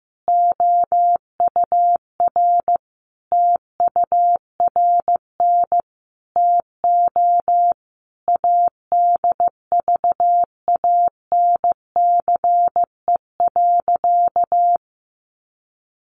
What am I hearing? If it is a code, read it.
Morse code, heard as a keyed tone, about 15 wpm, 700 Hz. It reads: OUR TURN TO ADVANCE.
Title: Sentences from Top 1400 Words 15wpm